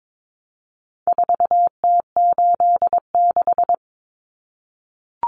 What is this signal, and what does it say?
Morse code, heard as a keyed tone, about 22 wpm, 700 Hz. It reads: 4T86
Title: Common Cut Numbers - 4-Digit 22wpm